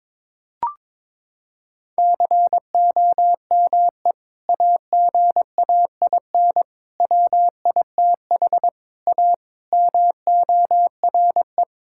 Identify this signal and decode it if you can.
Morse code, heard as a keyed tone, about 22 wpm, 700 Hz. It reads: COME AGAIN WITH A MORE
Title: Sentences from Top 1000 Words - Review Part 6 - 22wpm